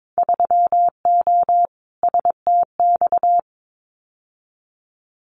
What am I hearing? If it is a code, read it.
Morse code, heard as a keyed tone, about 22 wpm, 700 Hz. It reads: 3O STX
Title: Winter Field Day Exchanges 22wpm